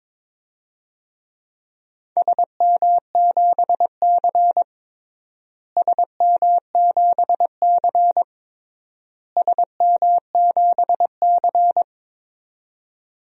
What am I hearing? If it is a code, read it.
Morse code, heard as a keyed tone, about 22 wpm, 700 Hz. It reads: SM7C SM7C SM7C
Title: DXCC-100 Call Signs - 4 Characters - Repeated 3x - 22wpm